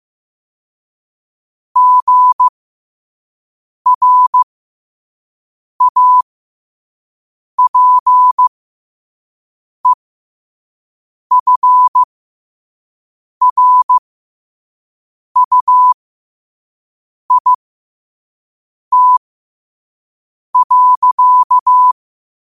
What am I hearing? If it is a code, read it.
Morse code, heard as a keyed tone, about 15 wpm, spaced as for 5 wpm, 1000 Hz. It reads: GRAPEFRUIT.
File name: qotdfiles / 5wpm